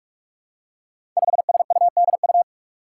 Morse code, heard as a keyed tone, about 44 wpm, 700 Hz. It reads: 5SUDU
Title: Five Challenging Letter-Number Combinations 45wpm